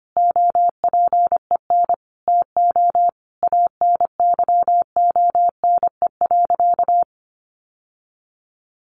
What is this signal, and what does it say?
Morse code, heard as a keyed tone, about 25 wpm, 700 Hz. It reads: OPEN TO ANYONE.
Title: Sentences from Top 1200 Words 25wpm